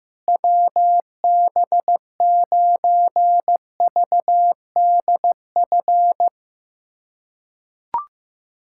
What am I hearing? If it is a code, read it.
Morse code, heard as a keyed tone, about 15 wpm, 700 Hz. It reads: WB9VDF
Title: USA Callsign DE Callsign - 6 Characters - 15wpm